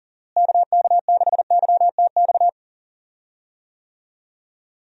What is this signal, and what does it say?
Morse code, heard as a keyed tone, about 39 wpm, 700 Hz. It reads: KK6YTX
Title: USA Call Signs - 6 Characters - Repeated 2x - 40wpm